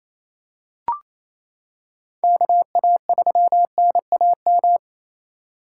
Morse code, heard as a keyed tone, about 28 wpm, 700 Hz. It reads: KA3NAM
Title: K1USN SST QSO Practice - 28wpm